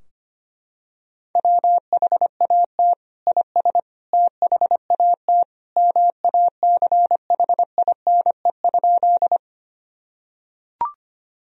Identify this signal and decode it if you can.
Morse code, heard as a keyed tone, about 25 wpm, 700 Hz. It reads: WHAT IS THAT MACHINE?
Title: Sentences from Top 1000 Words - Review Part 2 - 25wpm